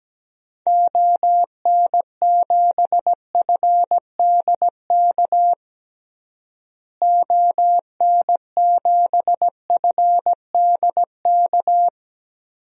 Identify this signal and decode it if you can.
Morse code, heard as a keyed tone, about 17 wpm, 700 Hz. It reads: ON7FDK ON7FDK
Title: DXCC-100 Call Signs - 6 Characters - Repeated 3x - 17wpm